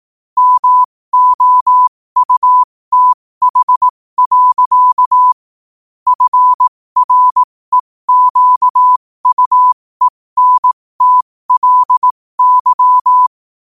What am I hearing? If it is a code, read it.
Morse code, heard as a keyed tone, about 18 wpm, spaced as for 15 wpm, 1000 Hz. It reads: MOUTH. FREQUENTLY
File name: qotdfiles / 15wpm